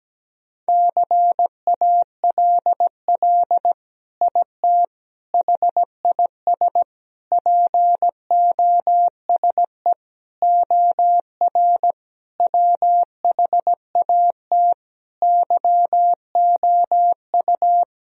Morse code, heard as a keyed tone, about 17 wpm, 700 Hz. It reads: CALL IT HIS POSE OR WHAT YOU
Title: Sentences from Top 1000 Words - Review Part 2 - 17wpm